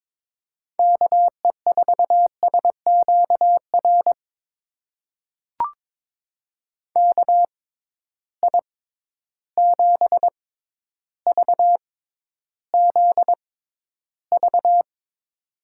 Morse code, heard as a keyed tone, about 22 wpm, spaced as for 11 wpm, 700 Hz. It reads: KE4SQR KI7VZV
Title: USA Call Signs - 6 Characters - 6x Character Spacing - 22wpm